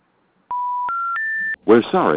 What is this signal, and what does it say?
Special information tone, with speech.